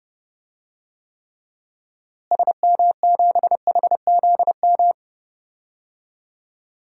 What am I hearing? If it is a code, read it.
Morse code, heard as a keyed tone, about 30 wpm, 700 Hz. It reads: SM7HZM
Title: DXCC-100 Call Signs - 6 Characters - Repeated 3x - 30wpm